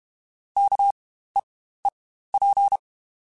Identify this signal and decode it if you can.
Morse code, heard as a keyed tone, about 32 wpm, spaced as for 15 wpm, 800 Hz. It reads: KEEP
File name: Spaced / Chuckles Spaced-mp3s / SoftWords sp @30